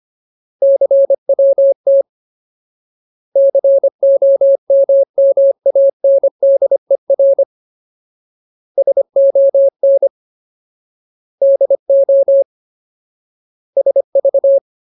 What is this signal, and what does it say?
Morse code, heard as a keyed tone, about 25 wpm, 550 Hz. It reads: CWT COMMANDER SON DO SV
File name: Ultimate Word List / 25WPM Ultimate